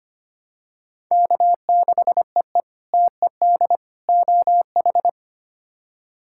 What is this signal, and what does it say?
Morse code, heard as a keyed tone, about 25 wpm, 700 Hz. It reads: K6EE TED OH
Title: K1USN SST QSO Practice - 25wpm